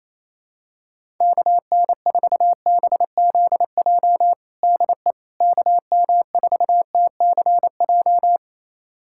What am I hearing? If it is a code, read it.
Morse code, heard as a keyed tone, about 28 wpm, 700 Hz. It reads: KN4BZJ DE KM4TCJ
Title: USA Callsign DE Callsign - 6 Characters - 28wpm